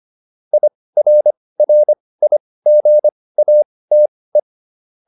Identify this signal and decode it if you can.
Morse code, heard as a keyed tone, about 25 wpm, spaced as for 18 wpm, 600 Hz. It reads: IRRIGATE